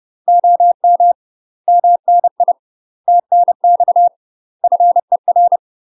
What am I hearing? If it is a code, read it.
Morse code, heard as a keyed tone, about 30 wpm, 700 Hz. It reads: OM MNI TNX FER
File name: UK MASTER QSOs TRAINING / 30WPM / UK MASTER QSOs 0038